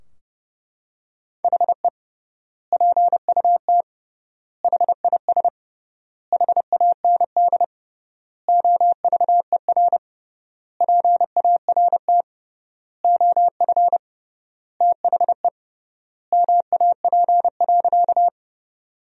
Morse code, heard as a keyed tone, about 32 wpm, 700 Hz. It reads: HE PUT HIS HAND OVER PART OF THE MAP.
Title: 3x Word-Spacing - Sentences from Top 400 Words - 30wpm